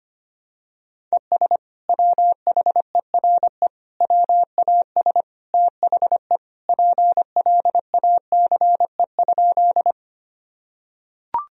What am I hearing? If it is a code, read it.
Morse code, heard as a keyed tone, about 25 wpm, 700 Hz. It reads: ES WHERE WAS THE PLACE?
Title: Sentences from Top 200 Words and Abbreviations 25wpm